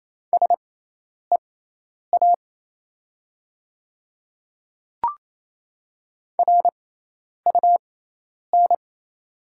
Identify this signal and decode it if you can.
Morse code, heard as a keyed tone, about 28 wpm, spaced as for 10 wpm, 700 Hz. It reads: SEA RUN
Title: 6x Character Spacing - Top 300 Words - 28wpm